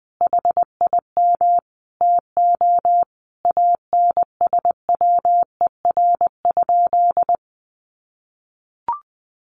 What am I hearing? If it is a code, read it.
Morse code, heard as a keyed tone, about 20 wpm, 700 Hz. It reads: HIM TO ANSWER?
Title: Sentences from Top 1100 Words 20wpm